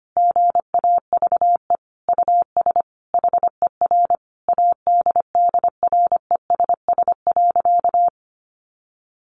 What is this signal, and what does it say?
Morse code, heard as a keyed tone, about 25 wpm, 700 Hz. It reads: GAVE US HER ADDRESS.
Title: Sentences from Top 1100 Words 25wpm